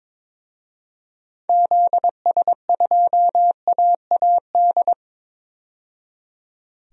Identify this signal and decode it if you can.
Morse code, heard as a keyed tone, about 22 wpm, 700 Hz. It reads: ZS2AAD
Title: DXCC-100 Call Signs - 6 Characters - 22wpm